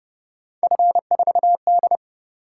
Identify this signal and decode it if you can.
Morse code, heard as a keyed tone, about 29 wpm, 700 Hz. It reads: F4D
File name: Speed-Racing - Three Letter-Number Combinations - 25wpm